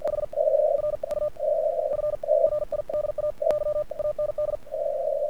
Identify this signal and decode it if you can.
Morse code, heard as a keyed tone, about 48 wpm, spaced as for 31 wpm, 1200 Hz. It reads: HR U FRIDAY AND
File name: archived / CRSnet